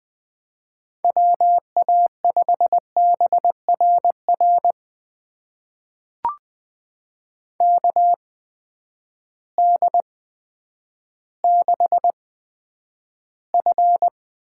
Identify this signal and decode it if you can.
Morse code, heard as a keyed tone, about 20 wpm, spaced as for 10 wpm, 700 Hz. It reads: WA5BRR KD6F
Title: USA Call Signs - 8x Character Spacing - 20wpm